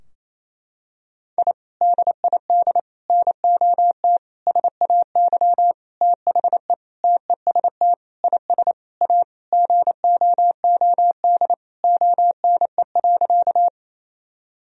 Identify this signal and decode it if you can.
Morse code, heard as a keyed tone, about 28 wpm, 700 Hz. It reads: I DID NOT SAY THE TEST IS A GOOD ONE.
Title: Sentences from Top 1000 Words - Review Part 3 - 28wpm